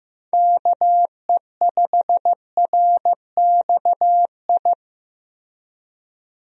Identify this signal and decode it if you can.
Morse code, heard as a keyed tone, about 15 wpm, 700 Hz. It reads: KE5RXI
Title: USA Call Signs - 6 Characters - Repeated 3x - 15wpm